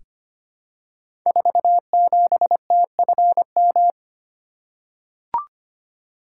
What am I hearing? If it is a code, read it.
Morse code, heard as a keyed tone, about 25 wpm, 700 Hz. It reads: 47TFM
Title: Code Groups - 5 Letter:Numbers - 25wpm